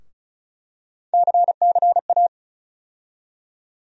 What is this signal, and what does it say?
Morse code, heard as a keyed tone, about 34 wpm, 700 Hz. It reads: CCA
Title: USA Call Sign Postfixes - 3 Letters 35wpm